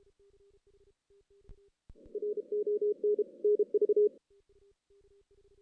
Morse code, heard as a keyed tone, about 30 wpm, 420 Hz. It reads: RON NV
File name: 2016 JAN CW NAQP / 2016 Jan naqp k5ot - 0-5